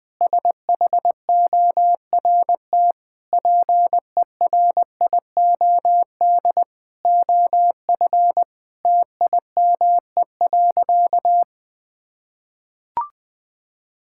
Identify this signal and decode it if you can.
Morse code, heard as a keyed tone, about 21 wpm, 700 Hz. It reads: SHORT PERIOD OF TIME.
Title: Sentences with 7-Word Phrases - Spoken Beginning - 20wpm